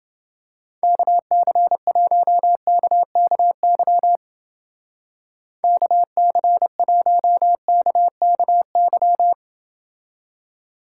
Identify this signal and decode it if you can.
Morse code, heard as a keyed tone, about 29 wpm, 700 Hz. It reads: KC1KKY KC1KKY